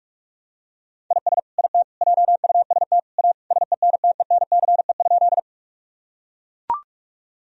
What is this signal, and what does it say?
Morse code, heard as a keyed tone, about 41 wpm, 700 Hz. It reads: IS IT JUST A SENTENCE?
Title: Sentences from Top 1000 Words - Review Part 1 - 45wpm